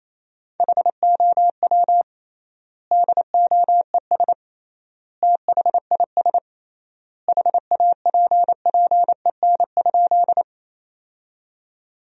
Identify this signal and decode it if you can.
Morse code, heard as a keyed tone, about 28 wpm, 700 Hz. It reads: HOW DOES THIS HAPPEN?
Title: 3x Word-Spacing - Sentences from Top 400 Words - 28wpm